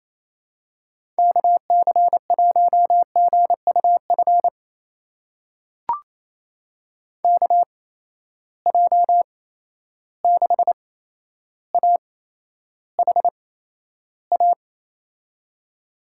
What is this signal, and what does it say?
Morse code, heard as a keyed tone, about 28 wpm, spaced as for 12 wpm, 700 Hz. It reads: KC1GUF KJ6AHA
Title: USA Call Signs - 6 Characters - 8x Character Spacing - 28wpm